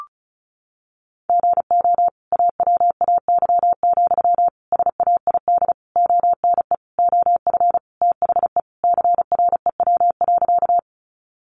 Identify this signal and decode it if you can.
Morse code, heard as a keyed tone, about 34 wpm, 700 Hz. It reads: GO AWAY, SAID ONE OF THE CREW.
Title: Sentences from Top 1500 Words 35wpm